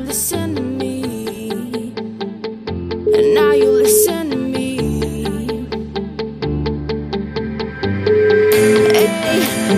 Ringback tone (Europe, 425 Hz), with noise.